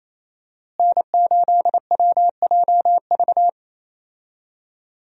Morse code, heard as a keyed tone, about 28 wpm, 700 Hz. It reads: N8WJV
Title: USA Call Signs - 5 Characters - Repeated 3x - 28wpm